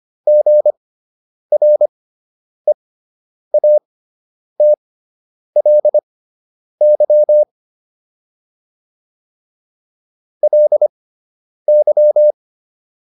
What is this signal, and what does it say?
Morse code, heard as a keyed tone, about 25 wpm, spaced as for 9 wpm, 600 Hz. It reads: GREATLY LY
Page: fundamental-practice-files